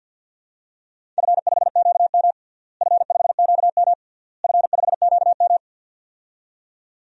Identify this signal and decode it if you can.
Morse code, heard as a keyed tone, about 50 wpm, 700 Hz. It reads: F5QM F5QM F5QM
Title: DXCC-100 Call Signs - 4 Characters - Repeated 3x - 50wpm